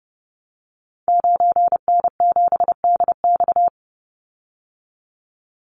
Morse code, heard as a keyed tone, about 30 wpm, 700 Hz. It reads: 9N7DX
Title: Call Signs Repeated 2x - 30wpm